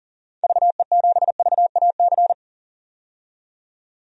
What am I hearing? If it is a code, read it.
Morse code, heard as a keyed tone, about 40 wpm, 700 Hz. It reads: VE7VAC
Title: DXCC-100 Call Signs - 6 Characters - 40wpm